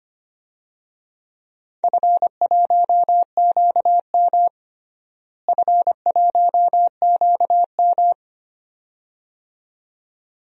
Morse code, heard as a keyed tone, about 25 wpm, 700 Hz. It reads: F1QM F1QM
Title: DXCC-100 Call Signs - 4 Characters - Repeated 2x - 25wpm